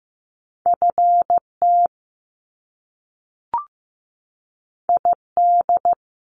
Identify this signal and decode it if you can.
Morse code, heard as a keyed tone, about 15 wpm, 700 Hz. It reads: FT ID
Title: Top Two Letter Combinations in Words 15wpm